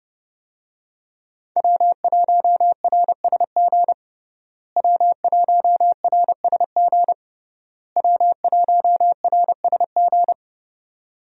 Morse code, heard as a keyed tone, about 29 wpm, 700 Hz. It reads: W1RSG W1RSG W1RSG
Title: USA Call Signs - 5 Characters - Repeated 3x - 30wpm